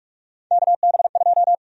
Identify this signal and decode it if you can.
Morse code, heard as a keyed tone, about 46 wpm, 700 Hz. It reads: KB2
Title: USA Call Sign Prefixes - Repeated 2x - 45wpm